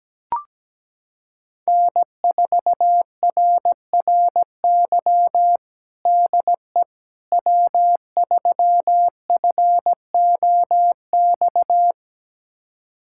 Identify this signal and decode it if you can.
Morse code, heard as a keyed tone, about 17 wpm, 700 Hz. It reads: N4RRY DE W3FOX
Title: USA Callsign DE Callsign - 5 Characters - 17wpm